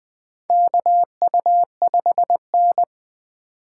Morse code, heard as a keyed tone, about 20 wpm, 700 Hz. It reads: KU5N
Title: USA Call Signs - 4 Characters - 4x Character Spacing - 20wpm